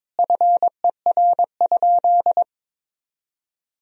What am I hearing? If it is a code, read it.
Morse code, heard as a keyed tone, about 22 wpm, 700 Hz. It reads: FER?
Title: Sentences from Top 200 Words and Abbreviations 22wpm